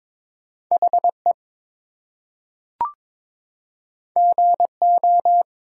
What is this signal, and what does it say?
Morse code, heard as a keyed tone, about 22 wpm, 700 Hz. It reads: HE GO
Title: Two Letter Words - 22wpm